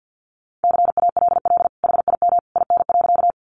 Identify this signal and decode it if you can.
Morse code, heard as a keyed tone, about 50 wpm, 700 Hz. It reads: CALL HIM IN.
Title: Sentences from Top 100 Words 50wpm